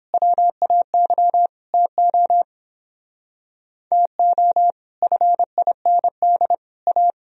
Morse code, heard as a keyed tone, about 29 wpm, 700 Hz. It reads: WAY TO TO FIND A